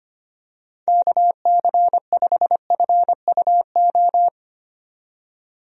Morse code, heard as a keyed tone, about 25 wpm, 700 Hz. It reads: KC5FUO
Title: USA Call Signs - 6 Characters - 25wpm